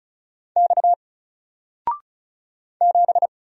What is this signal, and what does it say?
Morse code, heard as a keyed tone, about 35 wpm, 700 Hz. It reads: X 7
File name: Single-Letter-Number-35wpm